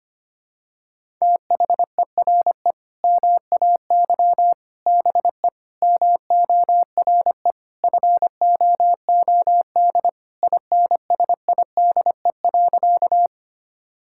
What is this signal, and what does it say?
Morse code, heard as a keyed tone, about 25 wpm, 700 Hz. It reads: THERE MAY BE MORE FOOD INSIDE.
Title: Sentences from Top 1000 Words - Review Part 6 - 25wpm